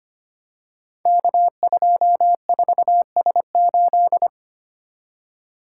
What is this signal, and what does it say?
Morse code, heard as a keyed tone, about 25 wpm, 700 Hz. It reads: K24S8
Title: Code Groups - 5 Letter:Numbers - 25wpm